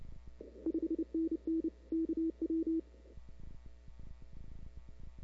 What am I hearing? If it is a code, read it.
Morse code, heard as a keyed tone, about 30 wpm, 320 Hz. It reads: 5NN KW
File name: CW / 2005-k5tr arrl cw dx -00:00